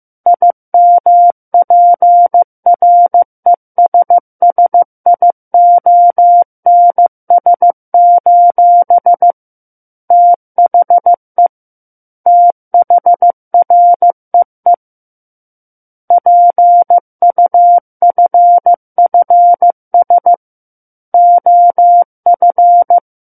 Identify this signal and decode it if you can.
Morse code, heard as a keyed tone, about 15 wpm, 700 Hz. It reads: IMPRESSIONS: THE THREE PUFFS OF
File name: Chapter 6 - 15 WPM